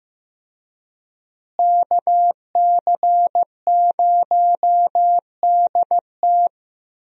Morse code, heard as a keyed tone, about 15 wpm, 700 Hz. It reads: KC0DT
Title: USA Call Signs - 5 Characters - Repeated 3x - 15wpm